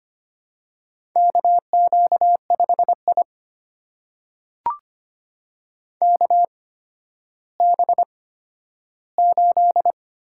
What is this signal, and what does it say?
Morse code, heard as a keyed tone, about 25 wpm, spaced as for 12 wpm, 700 Hz. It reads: KQ5I KB8
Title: USA Call Signs - 8x Character Spacing - 25wpm